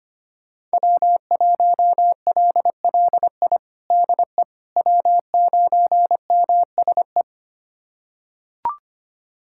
Morse code, heard as a keyed tone, about 25 wpm, 700 Hz. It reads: W1LLI DE W9MSE
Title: USA Callsign DE Callsign - 5 Characters - 25wpm